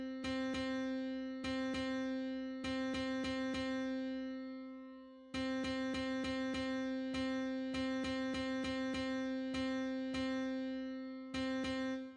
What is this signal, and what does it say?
There is no code.